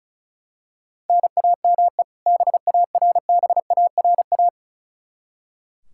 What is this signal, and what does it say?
Morse code, heard as a keyed tone, about 36 wpm, 700 Hz. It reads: NAME BARBARA
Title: Practice QSO - Exchange 1 - Beginning - 35wpm